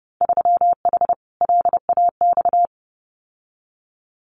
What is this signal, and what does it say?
Morse code, heard as a keyed tone, about 30 wpm, 700 Hz. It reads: 3H LAX
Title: Winter Field Day Exchanges 30wpm